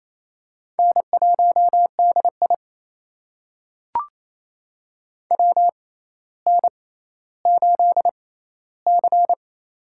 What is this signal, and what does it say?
Morse code, heard as a keyed tone, about 28 wpm, spaced as for 14 wpm, 700 Hz. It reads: N1DI WN8C
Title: USA Call Signs - 6x Character Spacing - 28wpm